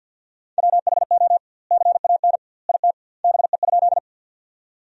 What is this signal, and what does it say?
Morse code, heard as a keyed tone, about 47 wpm, 700 Hz. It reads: WHO CAN IT BE?
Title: 2x Word-Spacing - Sentences from Top 100 Words - 50wpm